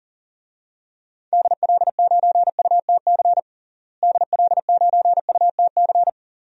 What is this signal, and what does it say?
Morse code, heard as a keyed tone, about 40 wpm, 700 Hz. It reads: DL9UTC DL9UTC
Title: DXCC-100 Call Signs - 6 Characters - Repeated 3x - 40wpm